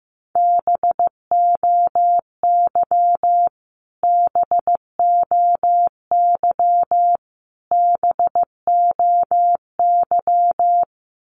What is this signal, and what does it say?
Morse code, heard as a keyed tone, about 15 wpm, 700 Hz. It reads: BOY BOY BOY
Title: Top 500 Words Repeated 15wpm